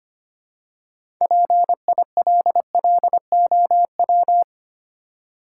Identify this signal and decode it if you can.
Morse code, heard as a keyed tone, about 25 wpm, 700 Hz. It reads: PILLOW
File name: practice / 25wpm / Easy Words with Hints - 25wpm